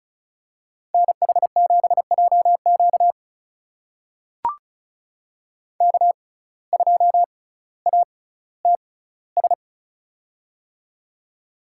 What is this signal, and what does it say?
Morse code, heard as a keyed tone, about 36 wpm, spaced as for 18 wpm, 700 Hz. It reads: NH7JQ K2ATS
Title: USA Call Signs - 5 Characters - 6x Character Spacing - 35wpm